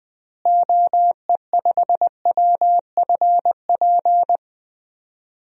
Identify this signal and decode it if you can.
Morse code, heard as a keyed tone, about 20 wpm, 700 Hz. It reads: OE5WFP
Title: DXCC-100 Call Signs - 6 Characters - Repeated 2x - 20wpm